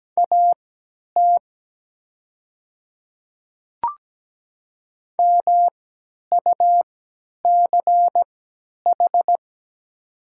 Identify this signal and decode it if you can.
Morse code, heard as a keyed tone, about 17 wpm, spaced as for 10 wpm, 700 Hz. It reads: AT MUCH
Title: 3x Character Spacing - Top 200 Words - 17wpm